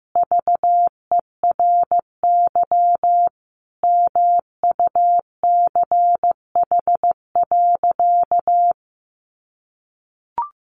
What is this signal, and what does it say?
Morse code, heard as a keyed tone, about 15 wpm, 700 Hz. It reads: VERY MUCH.
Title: Sentences from Top 600 Words 15wpm